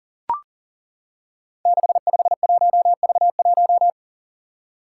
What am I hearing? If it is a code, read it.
Morse code, heard as a keyed tone, about 38 wpm, 700 Hz. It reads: 651V1
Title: Five Challenging Letter-Number Combinations 40wpm